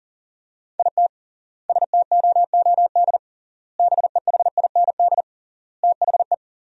Morse code, heard as a keyed tone, about 40 wpm, 700 Hz. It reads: IT STOOD BEHIND THE
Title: 3x Word-Spacing - Sentences from Top 500 Words - 40wpm